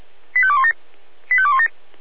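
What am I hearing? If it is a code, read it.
Five-tone selective call (ZVEI), heard as quick runs of five tones, tones of about 70 ms. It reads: 85217 85217
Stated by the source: ZVEI five-tone selective call, SDR reception.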